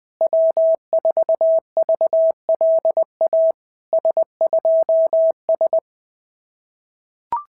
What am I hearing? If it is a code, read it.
Morse code, heard as a keyed tone, about 20 wpm, 650 Hz. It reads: W4VLA S2S
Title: SOTA QSO Practice 20wpm